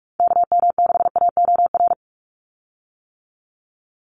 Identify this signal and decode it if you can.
Morse code, heard as a keyed tone, about 46 wpm, 700 Hz. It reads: KM6AOR
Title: USA Call Signs 45wpm